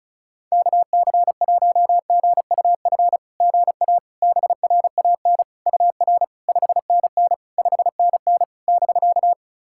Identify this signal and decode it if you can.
Morse code, heard as a keyed tone, about 36 wpm, 700 Hz. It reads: KC1GUF GA BRAN UR 5NN 5NN <BK>
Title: SOTA QSO Practice 35wpm